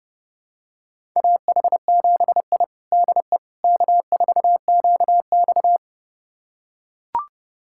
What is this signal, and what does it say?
Morse code, heard as a keyed tone, about 32 wpm, 700 Hz. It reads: AH7I DE K4QX
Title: USA Callsign DE Callsign - 4 Characters - 30wpm